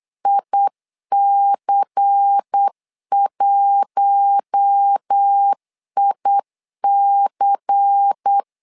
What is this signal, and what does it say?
Morse code, heard as a keyed tone, about 8 wpm, 800 Hz. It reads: IC1IC
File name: LEZIONI / 5 LBVC / 5 LBVC 8wpm 2935